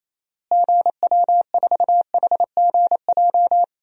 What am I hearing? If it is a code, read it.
Morse code, heard as a keyed tone, about 28 wpm, 700 Hz. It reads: GW4HGJ
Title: DXCC-100 Call Signs - 6 Characters - Repeated 2x - 28wpm